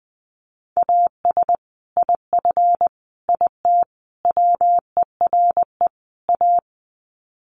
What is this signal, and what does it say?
Morse code, heard as a keyed tone, about 20 wpm, 700 Hz. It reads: AS IF IT WERE A
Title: Sets of 5 Words 20wpm